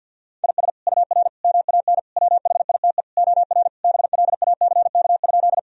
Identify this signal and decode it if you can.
Morse code, heard as a keyed tone, about 50 wpm, 700 Hz. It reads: IS UR MAN WHITE OR BLACK?
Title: Sentences from Top 400 Words and Abbreviations 50wpm